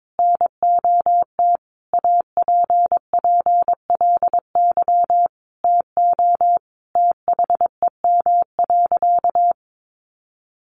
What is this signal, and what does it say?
Morse code, heard as a keyed tone, about 22 wpm, 700 Hz. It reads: NOT APPLY TO THEM.
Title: Sentences from Top 1200 Words 22wpm